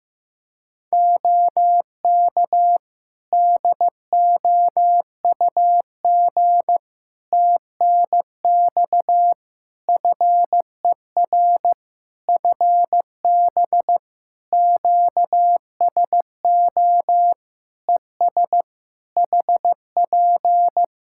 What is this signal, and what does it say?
Morse code, heard as a keyed tone, about 16 wpm, 700 Hz. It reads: OK DOUG TNX FER FB QSO ES HP
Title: Practice QSO - Exchange 4 - Ending - 15wpm